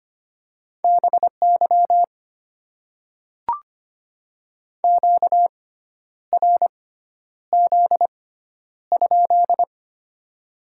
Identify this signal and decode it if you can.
Morse code, heard as a keyed tone, about 25 wpm, spaced as for 10 wpm, 700 Hz. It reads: BY QRZ?
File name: Top 90 QSO Elements - 6x Character Spacing - 25wpm